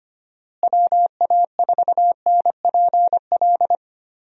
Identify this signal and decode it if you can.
Morse code, heard as a keyed tone, about 25 wpm, 700 Hz. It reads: WA4NPL
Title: USA Call Signs - 6 Characters - Repeated 3x - 25wpm